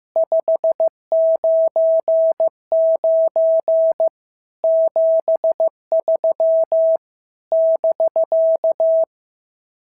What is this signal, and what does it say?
Morse code, heard as a keyed tone, about 15 wpm, 650 Hz. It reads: 599 73 <BK>
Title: SOTA QSO Practice 15wpm